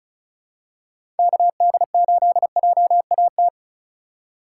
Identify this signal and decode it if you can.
Morse code, heard as a keyed tone, about 36 wpm, 700 Hz. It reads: KD8JAT
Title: USA Call Signs - 6 Characters - 3x Character Spacing - 35wpm